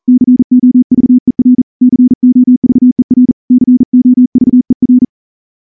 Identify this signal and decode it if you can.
Morse code, heard as a keyed tone, about 38 wpm, 250 Hz. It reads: COVER COVER COVER
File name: morse-training-40WPM